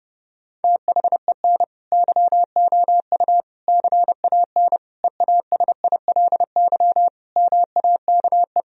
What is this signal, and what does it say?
Morse code, heard as a keyed tone, about 30 wpm, 700 Hz. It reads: THEN YOU CAN EASILY MAKE
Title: Sentences from Top 1000 Words - Review Part 1 - 30wpm